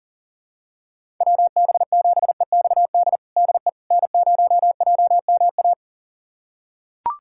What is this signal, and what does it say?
Morse code, heard as a keyed tone, about 40 wpm, 700 Hz. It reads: WB7EXD DE N0JMA
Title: USA Callsign DE Callsign 40wpm